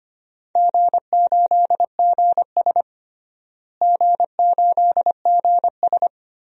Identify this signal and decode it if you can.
Morse code, heard as a keyed tone, about 25 wpm, 700 Hz. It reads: G8GS G8GS
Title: DXCC-100 Call Signs - 4 Characters - Repeated 3x - 25wpm